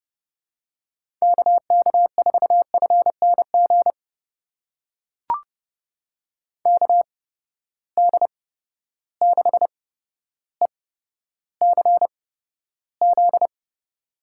Morse code, heard as a keyed tone, about 29 wpm, spaced as for 13 wpm, 700 Hz. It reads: KK4FNG KD6ECZ
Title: USA Call Signs - 6 Characters - 8x Character Spacing - 30wpm